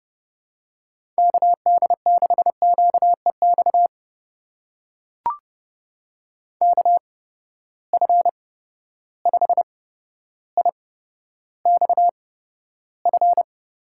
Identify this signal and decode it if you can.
Morse code, heard as a keyed tone, about 30 wpm, spaced as for 13 wpm, 700 Hz. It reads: KD6QEX KF5IXF
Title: USA Call Signs - 6 Characters - 8x Character Spacing - 30wpm